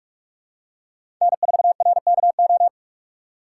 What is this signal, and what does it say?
Morse code, heard as a keyed tone, about 42 wpm, 700 Hz. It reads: N4RKO
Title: USA Call Signs - 5 Characters - 3x Character Spacing - 45wpm